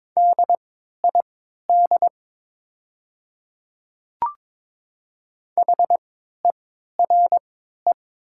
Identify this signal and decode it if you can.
Morse code, heard as a keyed tone, about 22 wpm, spaced as for 12 wpm, 700 Hz. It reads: DID HERE
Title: 3x Character Spacing - Top 300 Words - 22wpm